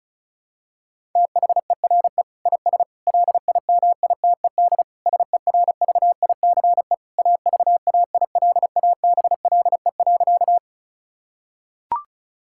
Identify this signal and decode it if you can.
Morse code, heard as a keyed tone, about 36 wpm, 700 Hz. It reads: THERE IS LIMITED SERVICE AVAILABLE.
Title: Sentences from Top 1000 Words - Review Part 6 - 35wpm